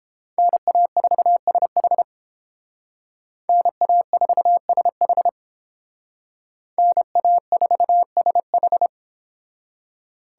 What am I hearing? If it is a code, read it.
Morse code, heard as a keyed tone, about 30 wpm, 700 Hz. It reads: NA4SH NA4SH NA4SH